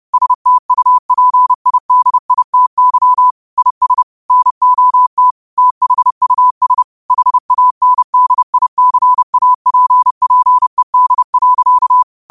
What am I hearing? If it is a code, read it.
Morse code, heard as a keyed tone, about 30 wpm, 1000 Hz. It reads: STUPIDITY IS NOT THUS HANDICAPPED.
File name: qotdfiles / 30wpm